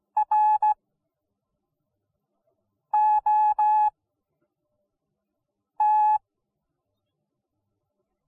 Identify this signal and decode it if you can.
Morse code, heard as a keyed tone, about 16 wpm, spaced as for 4 wpm, 850 Hz. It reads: ROT